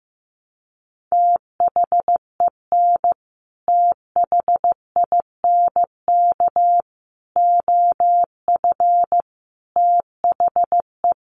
Morse code, heard as a keyed tone, about 15 wpm, 700 Hz. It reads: THEN THINK OF THE
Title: Sentences from Top 1000 Words 15wpm